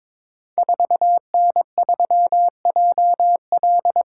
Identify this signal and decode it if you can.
Morse code, heard as a keyed tone, about 22 wpm, 700 Hz. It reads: 4N3JL
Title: Code Groups - 5 Letter:Numbers - 22wpm